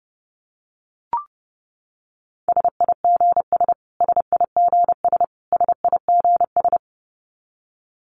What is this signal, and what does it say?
Morse code, heard as a keyed tone, about 30 wpm, 700 Hz. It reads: SIGS SIGS SIGS
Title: Abbreviations Repeated 3x - 30wpm